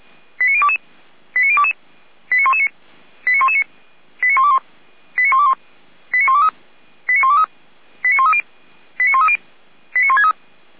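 Five-tone selective call (ZVEI), heard as quick runs of five tones, tones of about 70 ms. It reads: 89022 89022 89119 89119 89121 89121 89123 89123 89130 89130 89163
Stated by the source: ZVEI five-tone selective call received by SDR.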